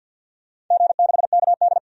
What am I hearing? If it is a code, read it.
Morse code, heard as a keyed tone, about 51 wpm, 700 Hz. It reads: G6KD
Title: DXCC-100 Call Signs - 4 Characters - 50wpm